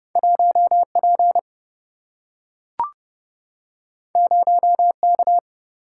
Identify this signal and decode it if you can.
Morse code, heard as a keyed tone, about 30 wpm, 700 Hz. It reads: 1P 0K
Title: Single-Letter Single-Number 30wpm